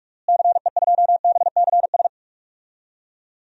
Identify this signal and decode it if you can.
Morse code, heard as a keyed tone, about 44 wpm, 700 Hz. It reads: CE2BCS
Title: DXCC-100 Call Signs - 6 Characters - Repeated 2x - 45wpm